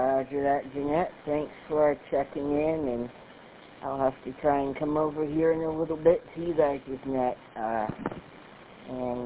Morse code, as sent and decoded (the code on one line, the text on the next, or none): none